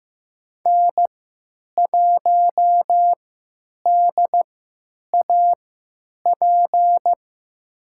Morse code, heard as a keyed tone, about 15 wpm, spaced as for 8 wpm, 700 Hz. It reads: N1DAP
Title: USA Call Signs - 3x Character Spacing - 15wpm